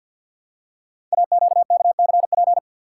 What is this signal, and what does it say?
Morse code, heard as a keyed tone, about 52 wpm, 700 Hz. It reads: AQKCP
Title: Code Groups - 5 Letters - 50wpm